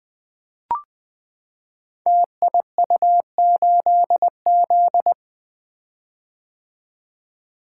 Morse code, heard as a keyed tone, about 20 wpm, 700 Hz. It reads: TIU8Z
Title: Code Groups - 5 Letter:Numbers - 20wpm